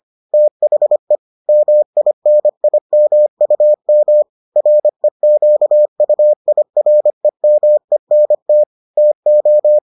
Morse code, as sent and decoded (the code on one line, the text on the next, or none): - .... / -- .. -. .. -- ..- -- / .-. --.- ..- .. .-. -- . -. - / - ---
THE MINIMUM REQUIREMENT TO